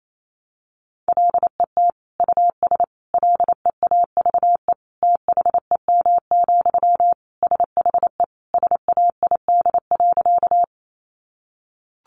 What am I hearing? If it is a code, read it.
Morse code, heard as a keyed tone, about 28 wpm, 700 Hz. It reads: LET US LEAVE THEM, SHE SAID.
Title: Sentences from Top 400 Words 28wpm